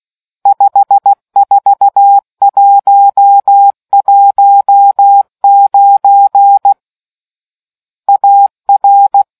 Morse code, heard as a keyed tone, about 16 wpm, 800 Hz. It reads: 54119 AR